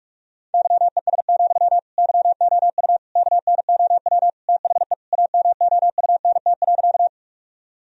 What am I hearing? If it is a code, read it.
Morse code, heard as a keyed tone, about 41 wpm, 700 Hz. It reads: YES, YOU KNOW THE AMOUNT.
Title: Sentences from Top 1000 Words - Review Part 4 - 45wpm